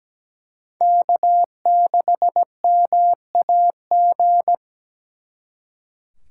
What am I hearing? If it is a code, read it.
Morse code, heard as a keyed tone, about 20 wpm, 700 Hz. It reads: K6MAG
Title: Speed Racing - USA 1x3 Call Signs 17wpm